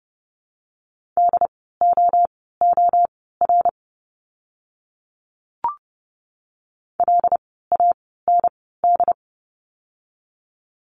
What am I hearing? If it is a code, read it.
Morse code, heard as a keyed tone, about 30 wpm, spaced as for 17 wpm, 700 Hz. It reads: DOOR LAND
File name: Lesson 17f - Letter u - Words 30wpm